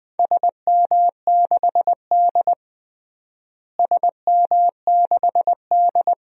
Morse code, heard as a keyed tone, about 20 wpm, 700 Hz. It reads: SM6D SM6D
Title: DXCC-100 Call Signs - 4 Characters - Repeated 2x - 20wpm